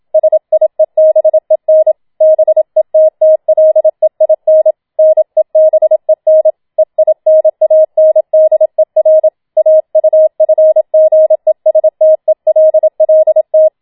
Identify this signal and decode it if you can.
Morse code, heard as a keyed tone, about 27 wpm, 630 Hz. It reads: SIEBEN BETTLEIN NEBEN EINANDER AUFGESTELLT